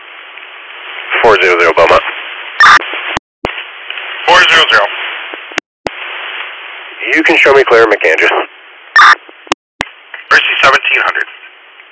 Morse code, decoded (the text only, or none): none